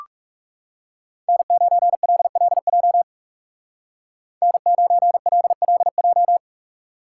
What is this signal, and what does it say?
Morse code, heard as a keyed tone, about 39 wpm, 700 Hz. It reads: N9LLJ N9LLJ